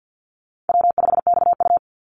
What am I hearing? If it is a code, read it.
Morse code, heard as a keyed tone, about 50 wpm, 700 Hz. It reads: W5XU
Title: Call Signs - 50wpm